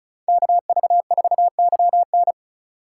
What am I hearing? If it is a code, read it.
Morse code, heard as a keyed tone, about 41 wpm, 700 Hz. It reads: KV4YN
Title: Speed Racing - USA 2x2 Call Signs 35wpm